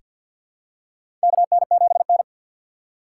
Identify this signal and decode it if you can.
Morse code, heard as a keyed tone, about 46 wpm, 700 Hz. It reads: KN7N
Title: USA Call Signs - 4 Characters - 2x Character Spacing - 50wpm